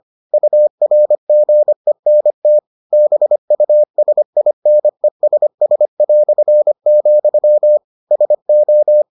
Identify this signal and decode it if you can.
Morse code, heard as a keyed tone, about 25 wpm, 600 Hz. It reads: URGENT BUSINESS", SO